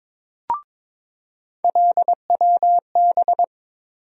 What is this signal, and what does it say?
Morse code, heard as a keyed tone, about 22 wpm, 700 Hz. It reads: LWB
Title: USA Call Sign Postfixes - 3 Letters 22wpm